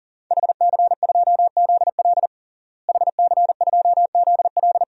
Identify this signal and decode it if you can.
Morse code, heard as a keyed tone, about 40 wpm, 690 Hz. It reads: HC2ZL HC2ZL
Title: DXCC-100 Call Signs - 5 Characters - Repeated 2x - 40wpm